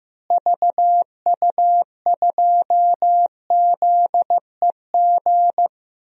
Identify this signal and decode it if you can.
Morse code, heard as a keyed tone, about 15 wpm, 700 Hz. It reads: VU2ZEG
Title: DXCC-100 Call Signs - 6 Characters - Repeated 3x - 15wpm